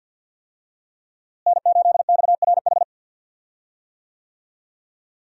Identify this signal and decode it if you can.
Morse code, heard as a keyed tone, about 52 wpm, 700 Hz. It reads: N8XRH
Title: USA Call Signs - 5 Characters - Repeated 3x - 50wpm